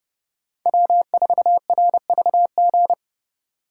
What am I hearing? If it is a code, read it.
Morse code, heard as a keyed tone, about 37 wpm, 700 Hz. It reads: W4RVG
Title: Speed Racing - USA 1x3 Call Signs 30wpm